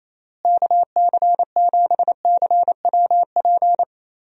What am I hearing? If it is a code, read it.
Morse code, heard as a keyed tone, about 28 wpm, 700 Hz. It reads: KC7CWP
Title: USA Call Signs - 6 Characters - Repeated 3x - 28wpm